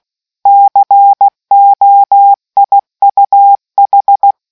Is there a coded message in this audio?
Yes: Morse code from 0.5 to 4.3 s, heard as a keyed tone, about 16 wpm, 800 Hz. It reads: COIUH